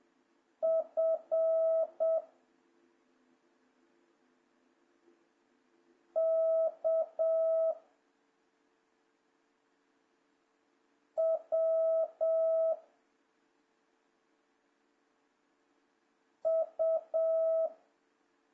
Morse code, heard as a keyed tone, about 7 wpm, 660 Hz. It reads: F K W U
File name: jh8cbh / 2026cw